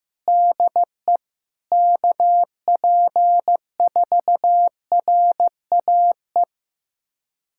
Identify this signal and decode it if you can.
Morse code, heard as a keyed tone, about 15 wpm, 700 Hz. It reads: DE KP4RAE
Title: USA Callsign DE Callsign - 6 Characters - 15wpm